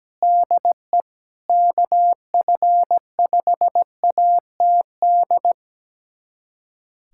Morse code, heard as a keyed tone, about 17 wpm, 700 Hz. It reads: DE KF5ATD
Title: USA Callsign DE Callsign - 6 Characters - 17wpm